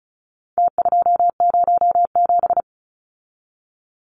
Morse code, heard as a keyed tone, about 34 wpm, 700 Hz. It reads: T207
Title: Numbers - Contest Style - 4-Digit 35wpm